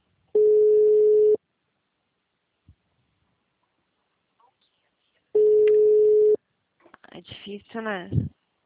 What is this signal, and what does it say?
Ringback tone (Europe, 425 Hz), with speech.